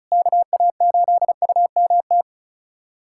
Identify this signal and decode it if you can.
Morse code, heard as a keyed tone, about 34 wpm, 700 Hz. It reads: KA8UMT
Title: USA Callsign DE Callsign - 6 Characters - 35wpm